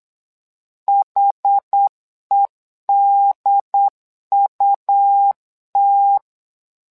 Morse code, heard as a keyed tone, about 8 wpm, 800 Hz. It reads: HEDUT
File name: LEZIONI / 2 UHDN / 2 UHD 8wpm 2258